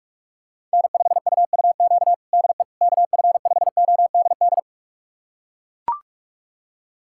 Morse code, heard as a keyed tone, about 42 wpm, 700 Hz. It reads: N5UUQ DE KF5ODD
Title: USA Callsign DE Callsign 45wpm